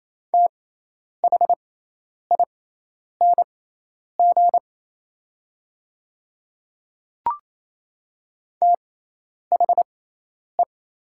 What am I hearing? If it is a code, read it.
Morse code, heard as a keyed tone, about 27 wpm, spaced as for 10 wpm, 700 Hz. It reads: THING THE
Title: 6x Character Spacing - Top 100 Words - 28wpm